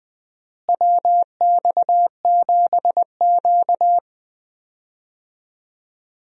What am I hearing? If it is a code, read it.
Morse code, heard as a keyed tone, about 20 wpm, 700 Hz. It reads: WX7Q
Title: USA Call Signs - 4 Characters - Repeated 3x - 20wpm